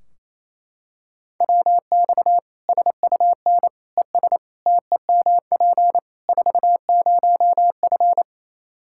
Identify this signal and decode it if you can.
Morse code, heard as a keyed tone, about 27 wpm, 700 Hz. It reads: WX SUN ES TEMP 40F
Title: Practice QSO - Exchange 2 - Info - 28wpm